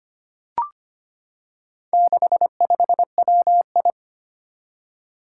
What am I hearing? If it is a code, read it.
Morse code, heard as a keyed tone, about 25 wpm, 700 Hz. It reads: 65WI
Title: Four Challenging Letter-Number Combinations 25wpm